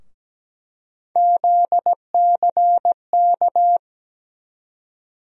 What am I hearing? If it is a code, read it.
Morse code, heard as a keyed tone, about 17 wpm, 700 Hz. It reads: ZCK
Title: USA Call Sign Postfixes - 3 Letters 17wpm